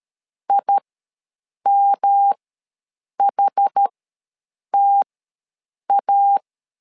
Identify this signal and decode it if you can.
Morse code, heard as a keyed tone, about 13 wpm, spaced as for 7 wpm, 800 Hz. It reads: IMHTA
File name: LEZIONI / 3 MWRO / 3 MWR ws 12wpm 1809